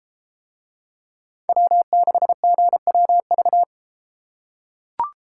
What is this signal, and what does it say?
Morse code, heard as a keyed tone, about 33 wpm, 700 Hz. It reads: W6GWV